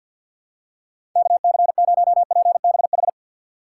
Morse code, heard as a keyed tone, about 49 wpm, 700 Hz. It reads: KC0PBH
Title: USA Call Signs - 6 Characters - 3x Character Spacing - 50wpm